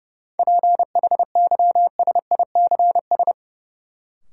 Morse code, heard as a keyed tone, about 30 wpm, 700 Hz. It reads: PHYSICS
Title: Sentences from Top 1500 Words - New Word Drill Alt - 30wpm